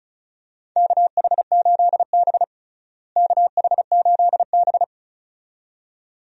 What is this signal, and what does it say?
Morse code, heard as a keyed tone, about 34 wpm, 700 Hz. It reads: KH8B KH8B
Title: USA Call Signs - 4 Characters - Repeated 2x - 35wpm